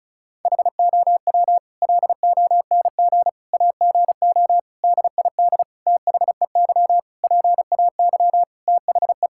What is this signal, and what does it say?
Morse code, heard as a keyed tone, about 34 wpm, 700 Hz. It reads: HOW LONG AGO DID THEY PAY THE
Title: Sentences from Top 1000 Words - Review Part 5 - 35wpm